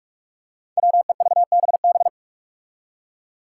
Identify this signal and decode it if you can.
Morse code, heard as a keyed tone, about 41 wpm, 700 Hz. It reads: WEVBB
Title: Five Challenging Letter-Number Combinations 45wpm